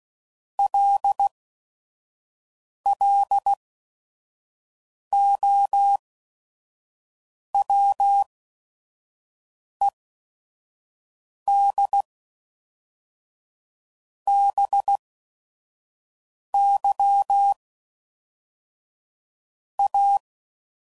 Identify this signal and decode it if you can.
Morse code, heard as a keyed tone, about 16 wpm, spaced as for 5 wpm, 800 Hz. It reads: LLOWED BY A
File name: CW Code Groups @5WPM